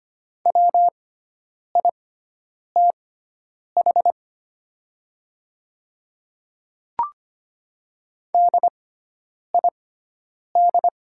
Morse code, heard as a keyed tone, about 25 wpm, spaced as for 9 wpm, 700 Hz. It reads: WITH DID
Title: 6x Character Spacing - Top 200 Words - 25wpm